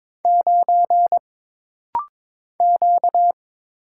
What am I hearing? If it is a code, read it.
Morse code, heard as a keyed tone, about 22 wpm, 700 Hz. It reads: 9 Q
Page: This rapid-fire timing is geared to push your ICR (Instant Character Recognition) skills.